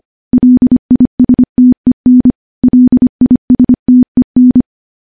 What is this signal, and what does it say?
Morse code, heard as a keyed tone, about 25 wpm, 250 Hz. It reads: LISTEN LISTEN
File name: morse / morse-training-25WPM